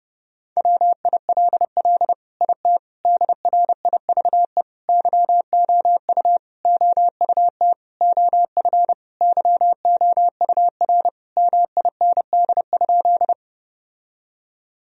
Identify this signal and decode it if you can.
Morse code, heard as a keyed tone, about 30 wpm, 700 Hz. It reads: WILL IT DRIVE YOU OUT OF YOUR MIND?
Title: Sentences from Top 1000 Words - Review Part 2 - 30wpm